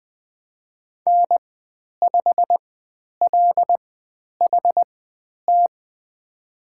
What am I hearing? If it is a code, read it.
Morse code, heard as a keyed tone, about 20 wpm, spaced as for 10 wpm, 700 Hz. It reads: N5LHT
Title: 095 Lesson 16g - Letter c - Call Signs 20wpm